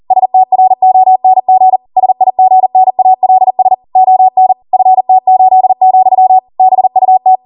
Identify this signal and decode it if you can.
Morse code, heard as a keyed tone, about 40 wpm, 750 Hz. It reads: STRONG SIGNALS ON FT8, BUT